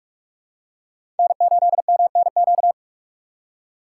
Morse code, heard as a keyed tone, about 47 wpm, 700 Hz. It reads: N8MNQ
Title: USA Call Signs - 5 Characters - 4x Character Spacing - 45wpm